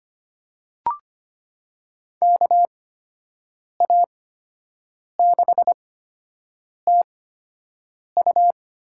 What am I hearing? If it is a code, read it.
Morse code, heard as a keyed tone, about 25 wpm, spaced as for 7 wpm, 700 Hz. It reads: KA6TU
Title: USA Call Signs - 8x Character Spacing - 25wpm